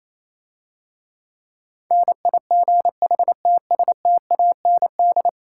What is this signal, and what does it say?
Morse code, heard as a keyed tone, about 27 wpm, 700 Hz. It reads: NIGHTSTAND
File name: Easy Words - Speed-Racing with Hints - 22wpm